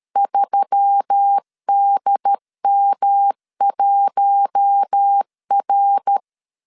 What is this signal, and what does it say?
Morse code, heard as a keyed tone, about 13 wpm, 800 Hz. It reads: 3DM1R
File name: CW / DET / 1 6 1500C S A 12wpm 3440